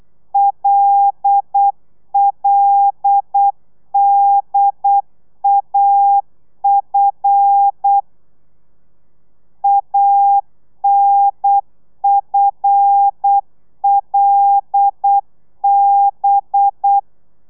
Morse code, heard as a keyed tone, about 8 wpm, 800 Hz. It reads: LLDAF ANFLB
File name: telegrafia / i2tfj / Lez5 35